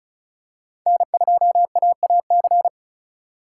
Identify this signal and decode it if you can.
Morse code, heard as a keyed tone, about 36 wpm, 700 Hz. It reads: N2AAC
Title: USA Call Signs - 5 Characters - 3x Character Spacing - 35wpm